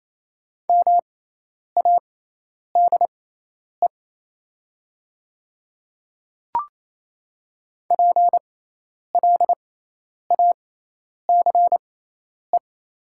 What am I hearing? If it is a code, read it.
Morse code, heard as a keyed tone, about 28 wpm, spaced as for 10 wpm, 700 Hz. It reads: MADE PLACE
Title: 6x Character Spacing - Top 200 Words - 28wpm